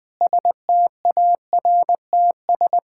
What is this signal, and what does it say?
Morse code, heard as a keyed tone, about 21 wpm, 700 Hz. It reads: STARTS
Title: Sentences with 7-Word Phrases - Spoken Beginning - 20wpm